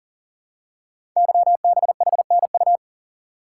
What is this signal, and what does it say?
Morse code, heard as a keyed tone, about 38 wpm, 690 Hz. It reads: YBHNU
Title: Code Groups - 5 Letter:Numbers - 40wpm